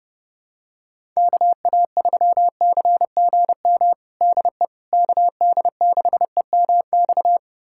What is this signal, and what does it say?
Morse code, heard as a keyed tone, about 30 wpm, 700 Hz. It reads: KA3CGM DE KD6EMX
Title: USA Callsign DE Callsign 30wpm